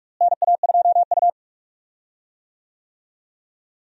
Morse code, heard as a keyed tone, about 46 wpm, 700 Hz. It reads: NA2U
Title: USA Call Signs - 4 Characters - Repeated 2x - 45wpm